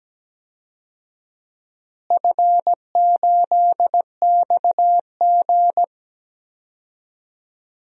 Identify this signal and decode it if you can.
Morse code, heard as a keyed tone, about 17 wpm, 700 Hz. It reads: F8XG
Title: DXCC-100 Call Signs - 4 Characters - Repeated 2x - 17wpm